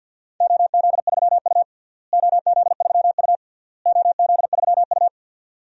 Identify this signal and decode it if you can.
Morse code, heard as a keyed tone, about 52 wpm, 700 Hz. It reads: OZ3U OZ3U OZ3U
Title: DXCC-100 Call Signs - 4 Characters - Repeated 3x - 50wpm